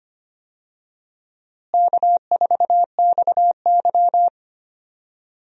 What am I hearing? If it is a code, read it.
Morse code, heard as a keyed tone, about 25 wpm, 700 Hz. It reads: K4XY
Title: USA Call Signs - 4 Characters - Repeated 3x - 25wpm